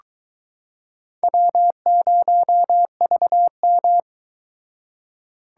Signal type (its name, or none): Morse code